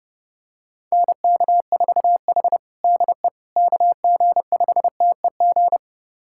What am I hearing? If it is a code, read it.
Morse code, heard as a keyed tone, about 30 wpm, 700 Hz. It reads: NK4H DE KG5TEG
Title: USA Callsign DE Callsign 30wpm